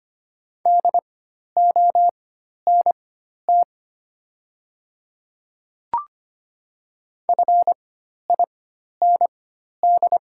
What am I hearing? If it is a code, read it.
Morse code, heard as a keyed tone, about 25 wpm, spaced as for 12 wpm, 700 Hz. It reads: DONT FIND
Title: 4x Character Spacing - Top 300 Words - 25wpm